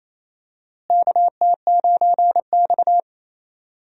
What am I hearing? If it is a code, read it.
Morse code, heard as a keyed tone, about 28 wpm, 700 Hz. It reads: KT9X
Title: USA Call Signs - 4 Characters - Repeated 2x - 28wpm